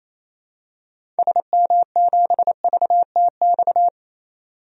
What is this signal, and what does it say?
Morse code, heard as a keyed tone, about 27 wpm, 700 Hz. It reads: SM7VTX
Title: DXCC-100 Call Signs - 6 Characters - 28wpm